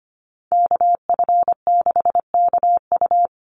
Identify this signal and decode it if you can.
Morse code, heard as a keyed tone, about 25 wpm, 690 Hz. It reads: KF6KU
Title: USA Call Signs 25wpm